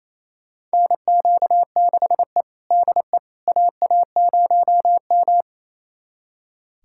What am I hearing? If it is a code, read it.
Morse code, heard as a keyed tone, about 28 wpm, 700 Hz. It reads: NQ6E DE AA0M
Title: USA Callsign DE Callsign - 4 Characters - 28wpm